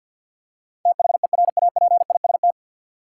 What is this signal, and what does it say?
Morse code, heard as a keyed tone, about 52 wpm, 700 Hz. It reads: THERAPIST
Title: K1USN SST QSO Practice 50wpm